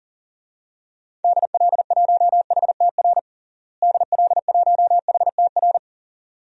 Morse code, heard as a keyed tone, about 37 wpm, 700 Hz. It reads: DL1HTR DL1HTR
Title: DXCC-100 Call Signs - 6 Characters - Repeated 2x - 40wpm